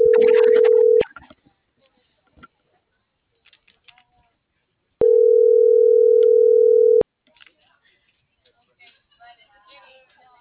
Ringback tone (North America), 440 + 480 Hz.